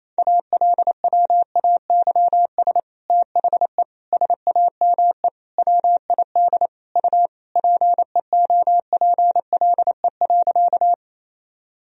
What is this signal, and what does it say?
Morse code, heard as a keyed tone, about 28 wpm, 700 Hz. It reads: ALWAYS THE SAME WID U PEOPLE.
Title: Sentences from Top 300 Words and Abbreviations 28wpm